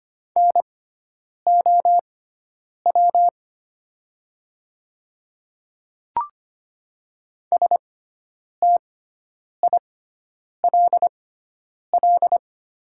Morse code, heard as a keyed tone, about 25 wpm, spaced as for 9 wpm, 700 Hz. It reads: NOW STILL
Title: 6x Character Spacing - Top 500 Words - 25wpm